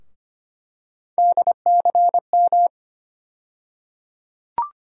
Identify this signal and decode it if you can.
Morse code, heard as a keyed tone, about 25 wpm, 700 Hz. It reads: DCM
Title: USA Call Sign Postfixes - 3 Letters 25wpm